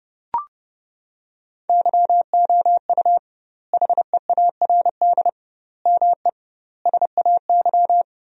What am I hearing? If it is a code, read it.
Morse code, heard as a keyed tone, about 32 wpm, 700 Hz. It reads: YOU HEARD ME SAY
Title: 2x Word-Spacing - Sentences from Top 500 Words - 30wpm